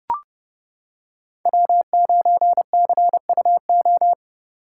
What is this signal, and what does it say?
Morse code, heard as a keyed tone, about 29 wpm, 700 Hz. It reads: W9CUO